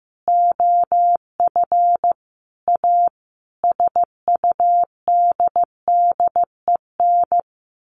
Morse code, heard as a keyed tone, about 15 wpm, 700 Hz. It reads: OF A SUDDEN
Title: Sets of 5 Words 15wpm